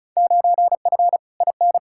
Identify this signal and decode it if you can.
Morse code, heard as a keyed tone, about 36 wpm, 700 Hz. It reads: 9F IN
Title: Field Day Exchanges Encore - 35wpm